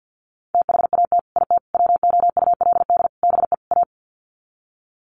Morse code, heard as a keyed tone, about 50 wpm, 700 Hz. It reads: THAT IT WOULD BE A
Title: Sets of 5 Words Encore2 50wpm